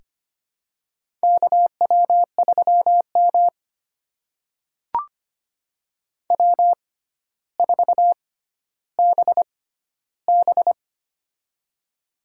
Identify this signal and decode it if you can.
Morse code, heard as a keyed tone, about 25 wpm, spaced as for 13 wpm, 700 Hz. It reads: KW3M W4BB
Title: USA Call Signs - 4 Characters - 6x Character Spacing - 25wpm